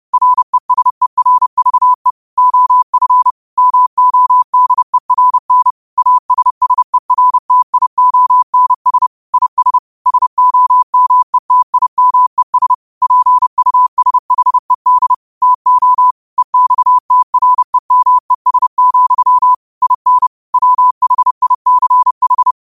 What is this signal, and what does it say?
Morse code, heard as a keyed tone, about 30 wpm, 1000 Hz. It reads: RESERVE OF MODERN ASSERTIONS IS SOMETIMES PUSHED TO EXTREMES, IN WHICH
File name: qotdfiles / 30wpm